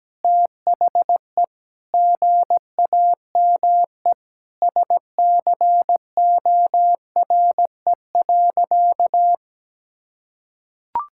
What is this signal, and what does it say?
Morse code, heard as a keyed tone, about 17 wpm, 700 Hz. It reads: THE GAME SCORE.
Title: Sentences from Top 1600 Words 17wpm